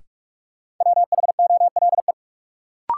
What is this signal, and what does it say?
Morse code, heard as a keyed tone, about 46 wpm, 700 Hz. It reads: WHOLE
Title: Sets of 2 Words - New Word Drill - 45wpm